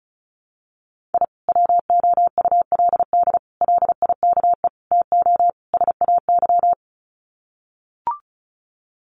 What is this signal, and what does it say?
Morse code, heard as a keyed tone, about 34 wpm, 700 Hz. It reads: I WOULD LIKE TO SAY
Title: Sets of 5 Words Encore2 35wpm